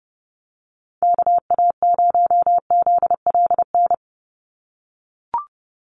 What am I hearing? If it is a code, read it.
Morse code, heard as a keyed tone, about 30 wpm, 700 Hz. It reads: KA0ZLN
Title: USA Call Signs 30wpm